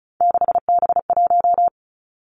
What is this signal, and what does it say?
Morse code, heard as a keyed tone, about 36 wpm, 700 Hz. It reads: 6B1
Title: Three Challenging Letter-Number Combinations 35wpm